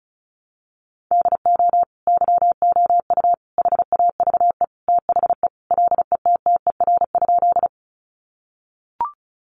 Morse code, heard as a keyed tone, about 34 wpm, 700 Hz. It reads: DO YOU HAVE THE LETTER?
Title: Sentences from Top 300 Words 35wpm